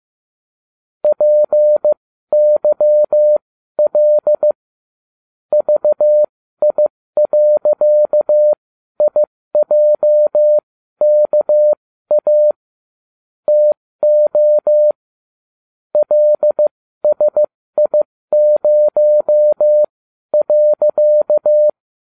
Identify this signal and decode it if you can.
Morse code, heard as a keyed tone, about 15 wpm, spaced as for 12 wpm, 600 Hz. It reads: PYL VI.IJKA TO LSI0.